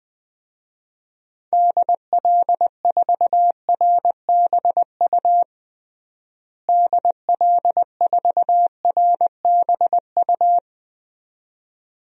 Morse code, heard as a keyed tone, about 21 wpm, 700 Hz. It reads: DL4RBU DL4RBU
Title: DXCC-100 Call Signs - 6 Characters - Repeated 2x - 20wpm